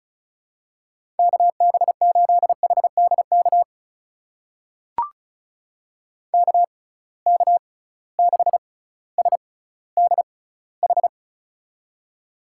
Morse code, heard as a keyed tone, about 36 wpm, spaced as for 18 wpm, 700 Hz. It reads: KB8HDK KK6SDH
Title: USA Call Signs - 6x Character Spacing - 35wpm